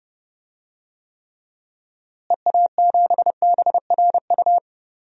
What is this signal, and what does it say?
Morse code, heard as a keyed tone, about 31 wpm, 700 Hz. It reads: EA7BRU
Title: DXCC-100 Call Signs - 6 Characters - Repeated 3x - 30wpm